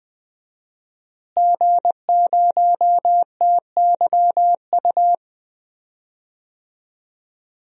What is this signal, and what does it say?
Morse code, heard as a keyed tone, about 20 wpm, 700 Hz. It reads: G0TYU
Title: DXCC-100 Call Signs - 5 Characters - Repeated 2x - 20wpm